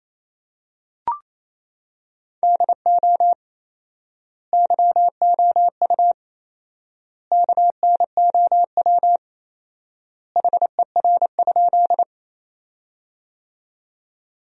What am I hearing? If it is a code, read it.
Morse code, heard as a keyed tone, about 28 wpm, 700 Hz. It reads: DO YOU KNOW HER?
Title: 4x Word-Spacing - Sentences from Top 100 Words - 28wpm